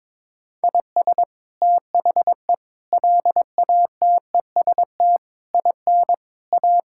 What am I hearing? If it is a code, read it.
Morse code, heard as a keyed tone, about 22 wpm, 700 Hz. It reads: IS THE LATEST IN A
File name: practice / 22wpm / Sets of 7 Words - 22wpm